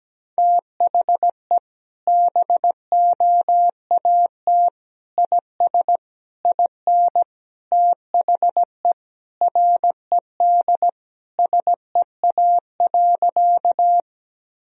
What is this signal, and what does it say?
Morse code, heard as a keyed tone, about 17 wpm, 700 Hz. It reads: THE BOAT IS IN THE RED SEA.
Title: Sentences from Top 1000 Words - Review Part 2 - 17wpm